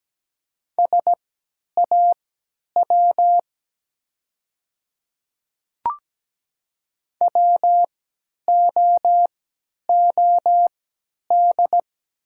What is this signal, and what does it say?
Morse code, heard as a keyed tone, about 17 wpm, spaced as for 10 wpm, 700 Hz. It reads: SAW WOOD
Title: 3x Character Spacing - Top 500 Words - 17wpm